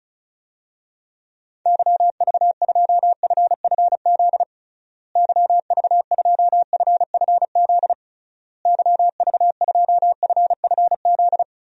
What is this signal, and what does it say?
Morse code, heard as a keyed tone, about 36 wpm, 700 Hz. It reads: YV2FFZ YV2FFZ YV2FFZ
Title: DXCC-100 Call Signs - 6 Characters - Repeated 3x - 35wpm